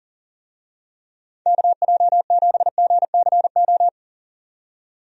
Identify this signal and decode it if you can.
Morse code, heard as a keyed tone, about 40 wpm, 700 Hz. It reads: KJ7GCO
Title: USA Call Signs - 6 Characters - 40wpm